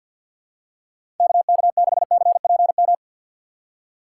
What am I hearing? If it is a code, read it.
Morse code, heard as a keyed tone, about 50 wpm, 700 Hz. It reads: KK6CPM
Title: USA Call Signs - 6 Characters - 4x Character Spacing - 50wpm